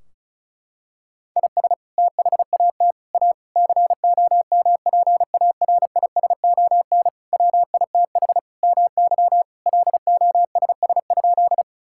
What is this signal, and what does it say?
Morse code, heard as a keyed tone, about 36 wpm, 700 Hz. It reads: IS THAT A COMPARISON WITH MY LOSS?
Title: Sentences from Top 1600 Words 35wpm